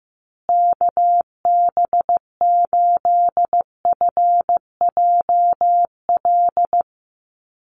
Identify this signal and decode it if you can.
Morse code, heard as a keyed tone, about 15 wpm, 700 Hz. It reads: KB8FJL
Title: USA Call Signs Repeated 3x 15wpm